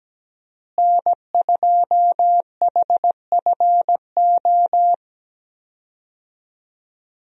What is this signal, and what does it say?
Morse code, heard as a keyed tone, about 17 wpm, 700 Hz. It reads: N2HFO
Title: USA Call Signs - 5 Characters - Repeated 2x - 17wpm